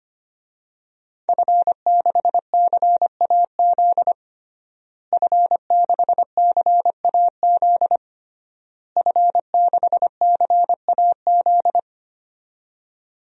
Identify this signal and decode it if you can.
Morse code, heard as a keyed tone, about 25 wpm, 700 Hz. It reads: F6CAZ F6CAZ F6CAZ
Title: DXCC-100 Call Signs - 5 Characters - Repeated 3x - 25wpm